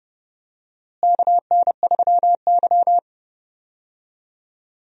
Morse code, heard as a keyed tone, about 30 wpm, 700 Hz. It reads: KN3Y
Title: USA Call Signs - 4 Characters - Repeated 3x - 30wpm